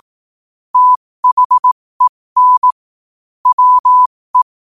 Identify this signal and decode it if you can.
Morse code, heard as a keyed tone, about 18 wpm, spaced as for 15 wpm, 1000 Hz. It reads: THEN WE
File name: qotdfiles / 15wpm